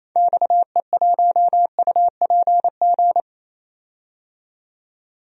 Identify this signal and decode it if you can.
Morse code, heard as a keyed tone, about 28 wpm, 700 Hz. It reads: XE1UPG
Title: DXCC-100 Call Signs - 6 Characters - 28wpm